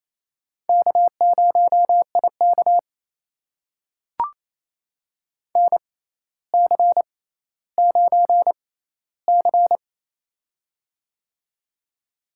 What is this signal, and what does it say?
Morse code, heard as a keyed tone, about 28 wpm, spaced as for 14 wpm, 700 Hz. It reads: K0IK NC9C
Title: USA Call Signs - 4 Characters - 6x Character Spacing - 28wpm